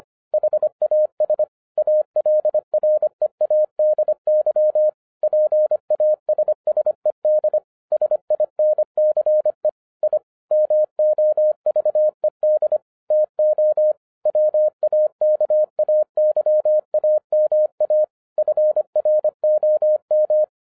Morse code, heard as a keyed tone, about 24 wpm, 610 Hz. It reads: HAS ALREADY PASSED SINCE I MOVED TO WAKAYAMA FROM
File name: 2011-10-mp3 / n04-JUST MOVED 25wpm